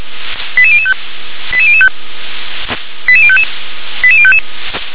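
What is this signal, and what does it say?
Five-tone selective call (ZVEI), heard as quick runs of five tones, tones of about 70 ms. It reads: 88005 88005 88055 88055
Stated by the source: ZVEI five-tone selective call received by SDR.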